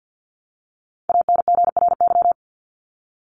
Morse code, heard as a keyed tone, about 48 wpm, 700 Hz. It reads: ANGRY
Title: Top 1751-2000 Words 50wpm